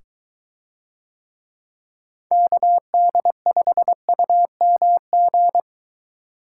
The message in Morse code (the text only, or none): KD5UMG